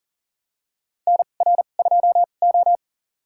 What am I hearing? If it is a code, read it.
Morse code, heard as a keyed tone, about 40 wpm, spaced as for 29 wpm, 700 Hz. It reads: NR2O
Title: USA Call Signs - 4 Characters - 2x Character Spacing - 40wpm